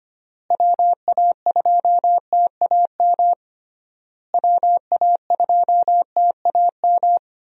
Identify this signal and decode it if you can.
Morse code, heard as a keyed tone, about 25 wpm, 700 Hz. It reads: WA2TAM WA2TAM
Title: USA Call Signs - 6 Characters - Repeated 3x - 25wpm